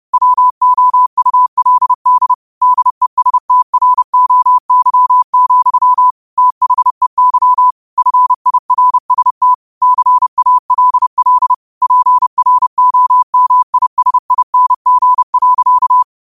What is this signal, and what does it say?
Morse code, heard as a keyed tone, about 30 wpm, 1000 Hz. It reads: WOULD DESTROY, THEY FIRST CALL PROMISING.
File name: qotdfiles / 30wpm